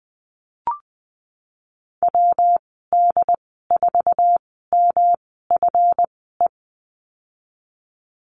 Morse code, heard as a keyed tone, about 20 wpm, spaced as for 15 wpm, 700 Hz. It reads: WD4MFE
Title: 173 Lesson 28g - Number 9 - Call Signs 20wpm